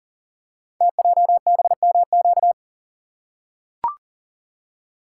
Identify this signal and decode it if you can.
Morse code, heard as a keyed tone, about 40 wpm, 700 Hz. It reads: TJBMQ
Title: Code Groups - 5 Letters - 40wpm